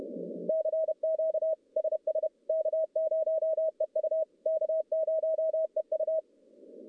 Morse code, heard as a keyed tone, about 31 wpm, 640 Hz. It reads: CQ SS K0EU K0EU
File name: contests / CW-SS-k5tr- -1200